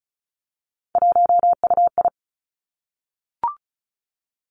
Morse code, heard as a keyed tone, about 36 wpm, 700 Hz. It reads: 1UI
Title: Three Challenging Letter-Number Combinations 35wpm